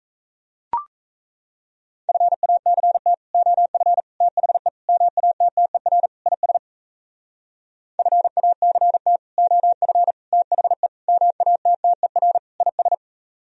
Morse code, heard as a keyed tone, about 36 wpm, 700 Hz. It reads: FACT OF THE MATTER IS FACT OF THE MATTER IS